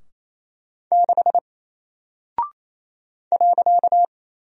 Morse code, heard as a keyed tone, about 28 wpm, 700 Hz. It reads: 6 .
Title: Single Character (Letter:Number:Punctuation:Prosign) Alt 28wpm